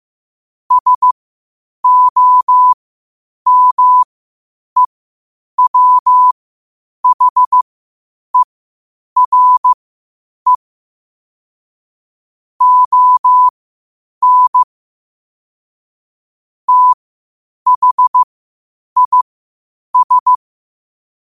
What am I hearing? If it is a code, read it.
Morse code, heard as a keyed tone, about 15 wpm, spaced as for 8 wpm, 1000 Hz. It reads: SOMEWHERE ON THIS
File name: qotdfiles / 8wpm